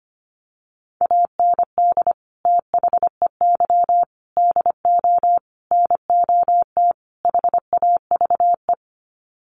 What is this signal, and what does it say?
Morse code, heard as a keyed tone, about 25 wpm, 700 Hz. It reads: AND THEY DO NOT HAVE
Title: Sets of 5 Words Encore2 25wpm